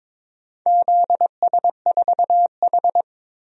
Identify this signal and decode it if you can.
Morse code, heard as a keyed tone, about 22 wpm, 700 Hz. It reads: ZS4H
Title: DXCC-100 Call Signs - 4 Characters - Repeated 3x - 22wpm